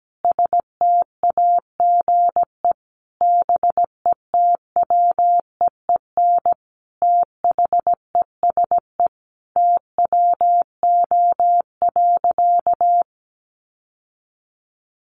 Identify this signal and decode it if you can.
Morse code, heard as a keyed tone, about 17 wpm, 700 Hz. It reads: STAGE BETWEEN THESE TWO.
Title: Sentences from Top 1200 Words 17wpm